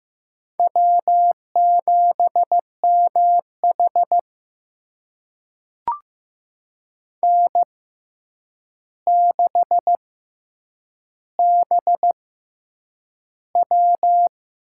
Morse code, heard as a keyed tone, about 15 wpm, spaced as for 8 wpm, 700 Hz. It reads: W7MH N6BW
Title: USA Call Signs - 6x Character Spacing - 15wpm